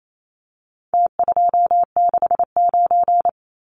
Morse code, heard as a keyed tone, about 28 wpm, 700 Hz. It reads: T269
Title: Numbers - Contest Style - 4-Digit 28wpm